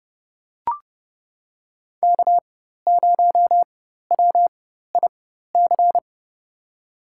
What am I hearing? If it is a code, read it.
Morse code, heard as a keyed tone, about 30 wpm, spaced as for 14 wpm, 700 Hz. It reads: K0WIC
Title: USA Call Signs - 5 Characters - 4x Character Spacing - 30wpm